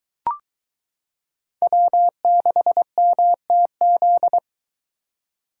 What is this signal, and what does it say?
Morse code, heard as a keyed tone, about 23 wpm, 700 Hz. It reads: W6MTZ